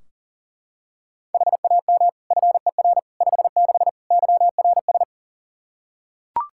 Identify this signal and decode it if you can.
Morse code, heard as a keyed tone, about 40 wpm, 700 Hz. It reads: HAM FER 56 YRS
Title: Practice QSO - Exchange 3 - More Info - 40wpm